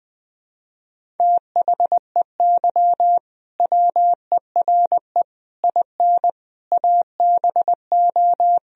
Morse code, heard as a keyed tone, about 20 wpm, 700 Hz. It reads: THEY WERE IN ABO
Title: Sentences from Top 1000 Words - Review Part 1 - 20wpm